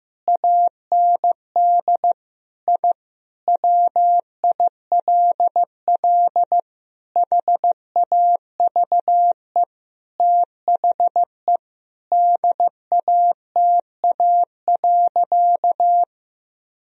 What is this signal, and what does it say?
Morse code, heard as a keyed tone, about 15 wpm, 700 Hz. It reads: AND I WILL HAVE THE DATA.
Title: Sentences from Top 1000 Words - Review Part 3 - 15wpm